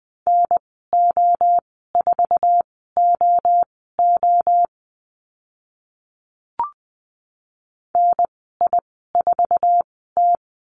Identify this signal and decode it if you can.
Morse code, heard as a keyed tone, about 20 wpm, spaced as for 14 wpm, 700 Hz. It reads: NO4OO NI4T
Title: 062 Lesson 11g - Letter h - Call Signs 20wpm